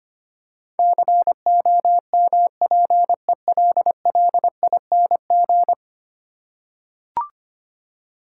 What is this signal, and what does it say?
Morse code, heard as a keyed tone, about 25 wpm, 700 Hz. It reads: COMPELLING
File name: 10 Letter Words - First 3 Letters Given - 25wpm